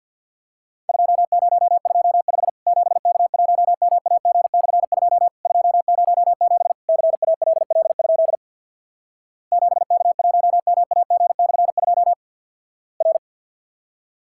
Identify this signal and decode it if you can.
Morse code, heard as a keyed tone, about 50 wpm, 700 Hz. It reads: 2025 7K1MAG/2 207 CALL? 7K1MAG/2 R
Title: Worked All Europe DX Contest - QTC Exchange with Repeats - 50wpm